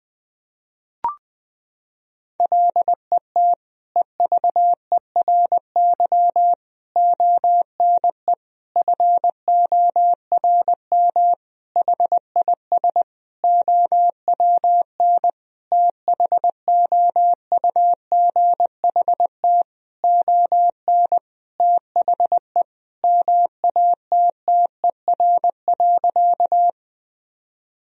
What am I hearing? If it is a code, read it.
Morse code, heard as a keyed tone, about 20 wpm, 700 Hz. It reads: LET EVERY ONE FORM HIS OWN THOUGHT ON THE MATTER.
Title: Sentences from Top 1000 Words - Review Part 1 - 20wpm